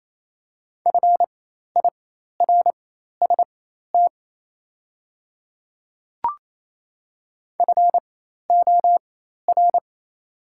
Morse code, heard as a keyed tone, about 28 wpm, spaced as for 13 wpm, 690 Hz. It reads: FIRST FOR
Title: 4x Character Spacing - Top 100 Words - 28wpm